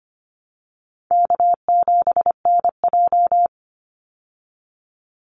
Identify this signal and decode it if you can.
Morse code, heard as a keyed tone, about 25 wpm, 700 Hz. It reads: K7NJ
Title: Call Signs Repeated 3x - 25wpm